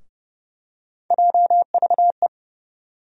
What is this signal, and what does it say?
Morse code, heard as a keyed tone, about 30 wpm, 700 Hz. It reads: JVE
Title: USA Call Sign Postfixes - 3 Letters 30wpm